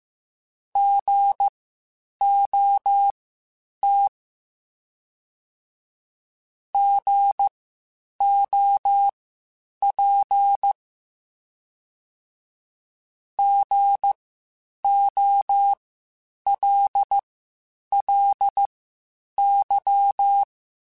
Morse code, heard as a keyed tone, about 15 wpm, spaced as for 8 wpm, 800 Hz. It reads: GOT GOP GOLLY
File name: Spaced Alphas / Spaced Alphas mp3s / Gs sp @13